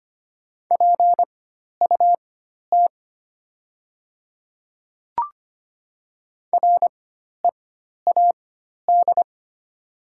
Morse code, heard as a keyed tone, about 25 wpm, spaced as for 12 wpm, 700 Hz. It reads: PUT READ
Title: 4x Character Spacing - Top 200 Words - 25wpm